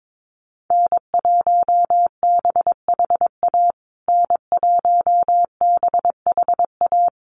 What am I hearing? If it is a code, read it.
Morse code, heard as a keyed tone, about 22 wpm, 700 Hz. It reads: N1BHA N1BHA
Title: USA Call Signs Repeated 3x 22wpm